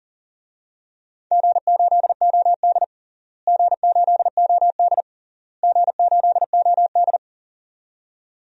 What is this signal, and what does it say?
Morse code, heard as a keyed tone, about 40 wpm, 700 Hz. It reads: G8OD G8OD G8OD
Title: DXCC-100 Call Signs - 4 Characters - Repeated 3x - 40wpm